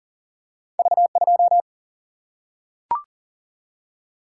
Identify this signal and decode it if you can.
Morse code, heard as a keyed tone, about 42 wpm, 700 Hz. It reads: V2
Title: Single-Letter Single-Number 40wpm